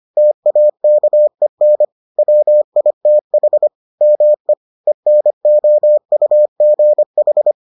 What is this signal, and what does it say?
Morse code, heard as a keyed tone, about 25 wpm, 600 Hz. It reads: TAKEN WITH ME ENOUGH